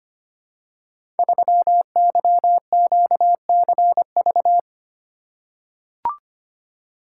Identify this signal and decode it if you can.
Morse code, heard as a keyed tone, about 25 wpm, 700 Hz. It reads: 3YQCV
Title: Code Groups - 5 Letter:Numbers - 25wpm